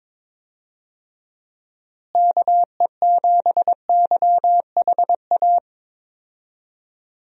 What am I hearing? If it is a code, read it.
Morse code, heard as a keyed tone, about 22 wpm, 700 Hz. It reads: KE7YHA
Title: USA Call Signs - 6 Characters - Repeated 3x - 22wpm